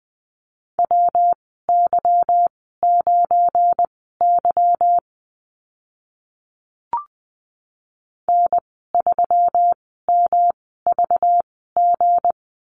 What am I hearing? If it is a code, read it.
Morse code, heard as a keyed tone, about 20 wpm, spaced as for 15 wpm, 700 Hz. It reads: WY9Y N3MVG
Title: 219 Lesson 35g - Number 0 - Call Signs 20wpm